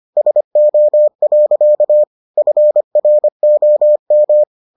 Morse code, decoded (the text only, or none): SO. FROM